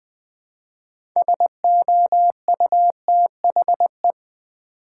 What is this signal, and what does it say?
Morse code, heard as a keyed tone, about 20 wpm, 700 Hz. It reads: SOUTHE
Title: Build A Word - 2 Letters At A Time - 8-Letter Words - 20wpm